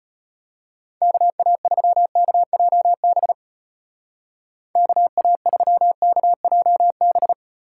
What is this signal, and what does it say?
Morse code, heard as a keyed tone, about 32 wpm, 700 Hz. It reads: KA3KJB KA3KJB